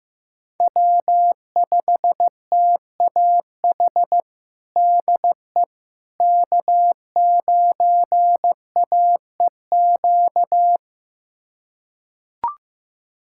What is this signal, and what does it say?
Morse code, heard as a keyed tone, about 15 wpm, 700 Hz. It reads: W5TAH DE K9AEQ
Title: USA Callsign DE Callsign - 5 Characters - 15wpm